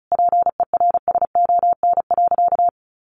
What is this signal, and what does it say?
Morse code, heard as a keyed tone, about 34 wpm, 700 Hz. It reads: PERSON.
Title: Sentences from Top 1300 Words 35wpm